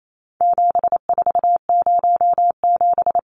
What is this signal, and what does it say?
Morse code, heard as a keyed tone, about 28 wpm, 700 Hz. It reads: 7407
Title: Numbers - 4-Digit 28wpm